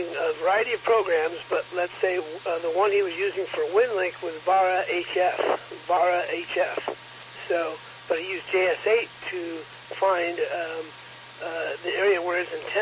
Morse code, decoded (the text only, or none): none